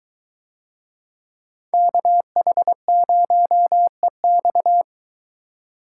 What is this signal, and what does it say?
Morse code, heard as a keyed tone, about 22 wpm, 700 Hz. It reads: KH0EX